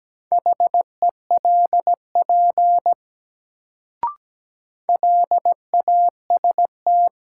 Morse code, heard as a keyed tone, about 17 wpm, 700 Hz. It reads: HELP LAST
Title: Sets of 2 Words - New Word Drill - 17wpm